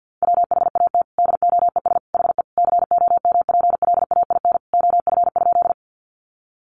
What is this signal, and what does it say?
Morse code, heard as a keyed tone, about 50 wpm, 700 Hz. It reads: WHAT DOES HE COMPLAIN OF?
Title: Sentences from Top 1300 Words 50wpm